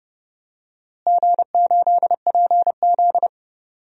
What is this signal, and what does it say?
Morse code, heard as a keyed tone, about 29 wpm, 700 Hz. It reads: G8PZ
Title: DXCC-100 Call Signs - 4 Characters - Repeated 2x - 30wpm